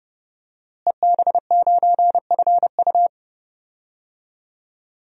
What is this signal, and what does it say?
Morse code, heard as a keyed tone, about 32 wpm, 700 Hz. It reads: EB9FU
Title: DXCC-100 Call Signs - 5 Characters - Repeated 3x - 30wpm